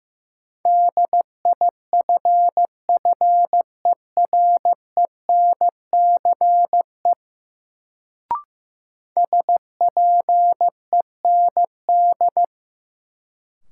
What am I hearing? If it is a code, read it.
Morse code, heard as a keyed tone, about 15 wpm, 700 Hz. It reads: DIFFERENCE SPEND
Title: Sentences from Top 800 Words - New Word Drill - 15wpm